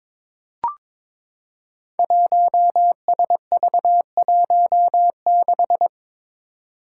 Morse code, heard as a keyed tone, about 21 wpm, 700 Hz. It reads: 1SV16
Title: Five Challenging Letter-Number Combinations 22wpm